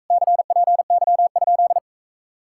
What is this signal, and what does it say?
Morse code, heard as a keyed tone, about 45 wpm, 700 Hz. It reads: CPY?